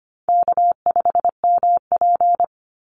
Morse code, heard as a keyed tone, about 25 wpm, 700 Hz. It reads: K5MP
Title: Call Signs Repeated 3x - 25wpm